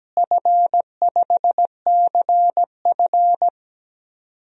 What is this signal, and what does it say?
Morse code, heard as a keyed tone, about 17 wpm, 700 Hz. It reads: F5CF
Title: DXCC-100 Call Signs - 4 Characters - Repeated 3x - 17wpm